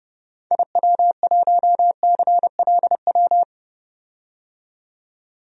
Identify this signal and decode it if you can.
Morse code, heard as a keyed tone, about 32 wpm, 700 Hz. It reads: IW1CLW
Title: DXCC-100 Call Signs - 6 Characters - 30wpm